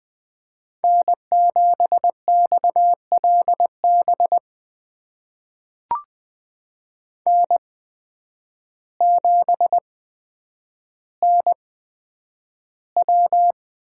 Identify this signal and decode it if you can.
Morse code, heard as a keyed tone, about 20 wpm, spaced as for 9 wpm, 700 Hz. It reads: N7XLB N7NW
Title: USA Call Signs - 5 Characters - 8x Character Spacing - 20wpm